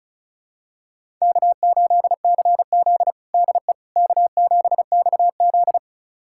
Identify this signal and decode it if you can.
Morse code, heard as a keyed tone, about 36 wpm, 700 Hz. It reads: K8CZ DE K7XZ
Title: USA Callsign DE Callsign - 4 Characters - 35wpm